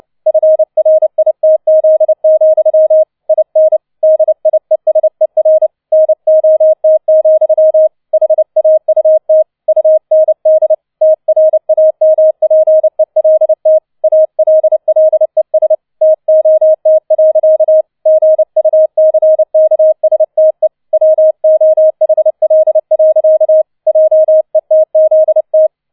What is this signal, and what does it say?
Morse code, heard as a keyed tone, about 29 wpm, 630 Hz. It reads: FRITZ, IN DIESER NOT, HAUT UND TRAMPELT ALLES TOT. GUCKSTE WOHL. JETZT